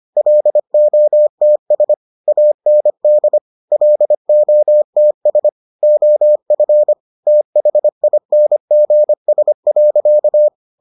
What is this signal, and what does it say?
Morse code, heard as a keyed tone, about 25 wpm, 600 Hz. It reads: LOTS AND LOTS OF THINGS.